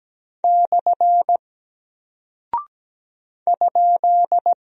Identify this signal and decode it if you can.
Morse code, heard as a keyed tone, about 17 wpm, 700 Hz. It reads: / ?
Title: Single Character (Letter:Number:Punctuation:Prosign) Alt 17wpm